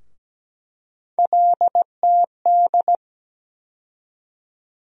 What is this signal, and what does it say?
Morse code, heard as a keyed tone, about 17 wpm, 700 Hz. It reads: LTD
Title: USA Call Sign Postfixes - 3 Letters 17wpm